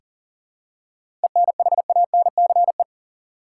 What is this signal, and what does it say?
Morse code, heard as a keyed tone, about 40 wpm, 700 Hz. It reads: ENHANCE
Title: Seven Letter Words - 40wpm